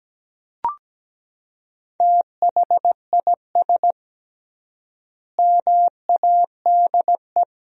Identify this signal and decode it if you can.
Morse code, heard as a keyed tone, about 17 wpm, 700 Hz. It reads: THIS MADE
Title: 3x Word-Spacing - Sentences from Top 500 Words - 17wpm